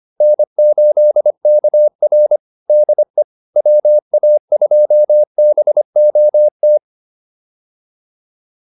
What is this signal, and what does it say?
Morse code, heard as a keyed tone, about 25 wpm, 600 Hz. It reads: N8KR DE WA2BOT